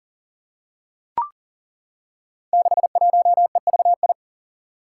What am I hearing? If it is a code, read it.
Morse code, heard as a keyed tone, about 40 wpm, 700 Hz. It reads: 61EVI
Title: Five Challenging Letter-Number Combinations 40wpm